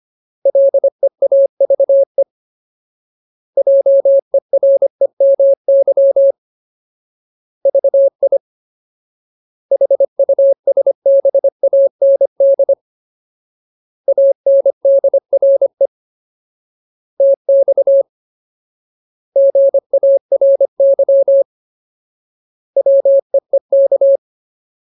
Morse code, heard as a keyed tone, about 25 wpm, 550 Hz. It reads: LEAVE JEREMY VI HUSBAND ANDRE TX GARY WEEK
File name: Ultimate Word List / 25WPM Ultimate